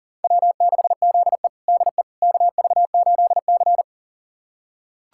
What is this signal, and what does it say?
Morse code, heard as a keyed tone, about 40 wpm, 700 Hz. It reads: W6ZE DE KV8C
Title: USA Callsign DE Callsign - 4 Characters - 40wpm